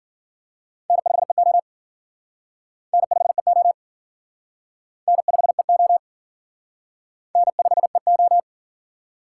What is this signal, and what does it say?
Morse code, heard as a keyed tone, about 53 wpm, 700 Hz. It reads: N5EO N5EO N5EO N5EO